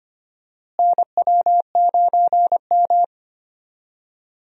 Morse code, heard as a keyed tone, about 25 wpm, 700 Hz. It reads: NW9M
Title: USA Call Signs - 4 Characters - Repeated 3x - 25wpm